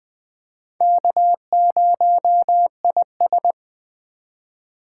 Morse code, heard as a keyed tone, about 20 wpm, 700 Hz. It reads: K0IS
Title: USA Call Signs - 4x Character Spacing - 20wpm